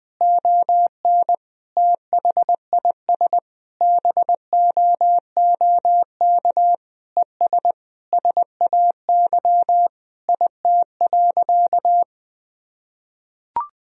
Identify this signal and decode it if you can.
Morse code, heard as a keyed tone, about 20 wpm, 700 Hz. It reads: ON THIS BOOK ES SAY IT.
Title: Sentences from Top 400 Words and Abbreviations 20wpm